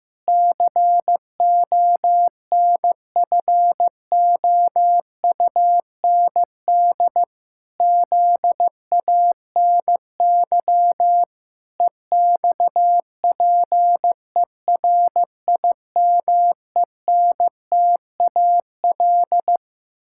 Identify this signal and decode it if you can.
Morse code, heard as a keyed tone, about 15 wpm, 700 Hz. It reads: CONFOUND ZANY EXPERIMENTAL
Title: Pangram Challenge v2 - 15wpm